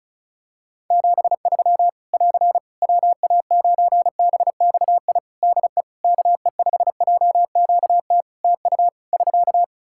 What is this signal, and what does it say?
Morse code, heard as a keyed tone, about 36 wpm, 700 Hz. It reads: 73 <AR> WA9BXI DE KE5JQT TU <SK>
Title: Practice QSO - Exchange 4 - Ending - 35wpm